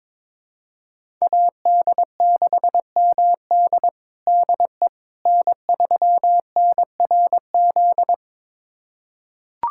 Morse code, heard as a keyed tone, about 22 wpm, 700 Hz. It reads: AD6MD DE N3NRZ
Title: USA Callsign DE Callsign - 5 Characters - 22wpm